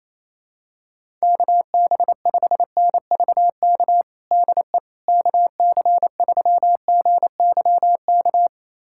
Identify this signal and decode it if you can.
Morse code, heard as a keyed tone, about 28 wpm, 700 Hz. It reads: KB5NVK DE KC3GYK
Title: USA Callsign DE Callsign 28wpm